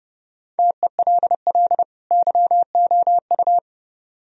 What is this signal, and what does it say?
Morse code, heard as a keyed tone, about 31 wpm, 700 Hz. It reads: TELL YOU